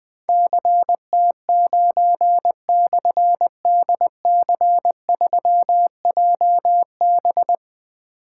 Morse code, heard as a keyed tone, about 20 wpm, 700 Hz. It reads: CT9/DC3JB
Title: Difficult Call Signs 20wpm